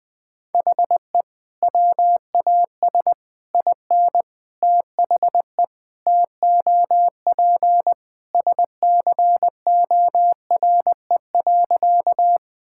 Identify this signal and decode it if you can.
Morse code, heard as a keyed tone, about 20 wpm, 700 Hz. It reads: HE WAS IN THE TOP SCORE.
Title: Sentences from Top 1600 Words 20wpm